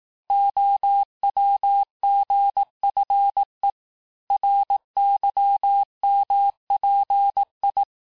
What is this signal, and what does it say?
Morse code, heard as a keyed tone, about 18 wpm, 800 Hz. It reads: OWGFE RYMPI
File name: cours-du-lundi files / cours-1588-texte-chiffre-vitesse-17MPM